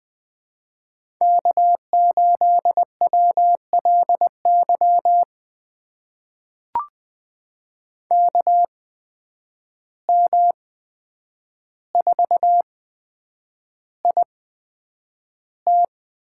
Morse code, heard as a keyed tone, about 20 wpm, spaced as for 9 wpm, 700 Hz. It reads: K8WLY KM4IT
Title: USA Call Signs - 5 Characters - 8x Character Spacing - 20wpm